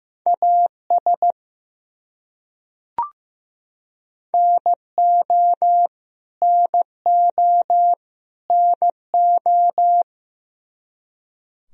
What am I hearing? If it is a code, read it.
Morse code, heard as a keyed tone, about 15 wpm, 700 Hz. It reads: AS NO NO NO
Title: Top 100 Words Repeated 15wpm